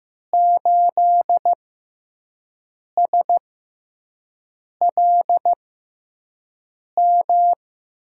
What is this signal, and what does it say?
Morse code, heard as a keyed tone, about 15 wpm, spaced as for 5 wpm, 700 Hz. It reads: 8SLM
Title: USA Call Signs - 5 Characters - 6x Character Spacing - 15wpm